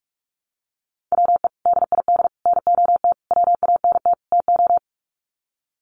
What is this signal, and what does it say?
Morse code, heard as a keyed tone, about 44 wpm, 700 Hz. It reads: WE DID NOT WANT TO
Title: Sets of 5 Words Encore 45wpm